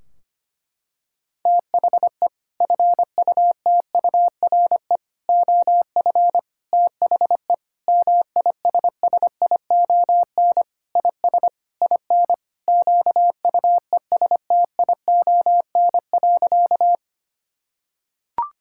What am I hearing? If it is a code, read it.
Morse code, heard as a keyed tone, about 25 wpm, 700 Hz. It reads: THE FUTURE OF THE MISSION IS IN QUESTION.
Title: Sentences from Top 1600 Words 25wpm